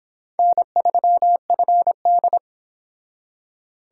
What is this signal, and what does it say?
Morse code, heard as a keyed tone, about 26 wpm, 700 Hz. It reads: N3FD